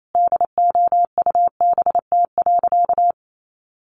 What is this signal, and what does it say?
Morse code, heard as a keyed tone, about 28 wpm, 700 Hz. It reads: DOUBT.
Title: Sentences from Top 1500 Words 28wpm